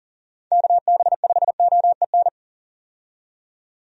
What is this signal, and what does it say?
Morse code, heard as a keyed tone, about 38 wpm, 700 Hz. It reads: KB5OEN
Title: USA Call Signs - 6 Characters - 40wpm